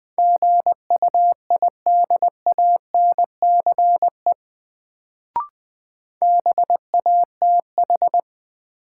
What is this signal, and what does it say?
Morse code, heard as a keyed tone, about 20 wpm, 700 Hz. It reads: GUIDANCE BATH
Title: Sentences from Top 1600 Words - New Word Drill Alt - 20wpm